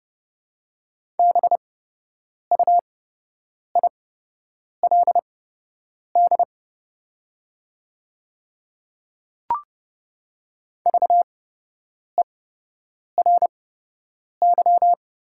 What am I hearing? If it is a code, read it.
Morse code, heard as a keyed tone, about 30 wpm, spaced as for 8 wpm, 700 Hz. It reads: BUILD VERY
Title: 8x Character Spacing - Top 200 Words - 30wpm